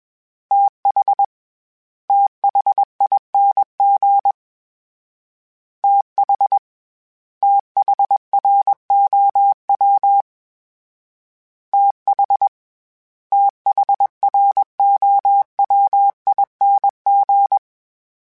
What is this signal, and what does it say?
Morse code, heard as a keyed tone, about 21 wpm, 800 Hz. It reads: TH THING TH THROW TH THROWING
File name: TH / TH mp3s / TH Prefixes 1x1 @20